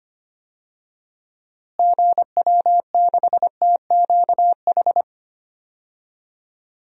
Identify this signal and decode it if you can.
Morse code, heard as a keyed tone, about 25 wpm, 700 Hz. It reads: GW6TQH
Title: DXCC-100 Call Signs - 6 Characters - Repeated 2x - 25wpm